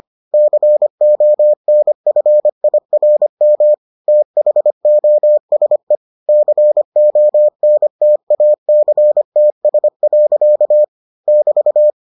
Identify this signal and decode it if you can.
Morse code, heard as a keyed tone, about 25 wpm, 600 Hz. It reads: CONFIRM THOSE CONTACTS. <BT>